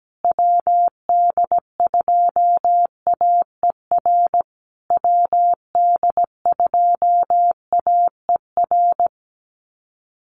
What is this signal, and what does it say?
Morse code, heard as a keyed tone, about 17 wpm, 700 Hz. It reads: WD2AER WD2AER
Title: USA Call Signs Repeated 3x 17wpm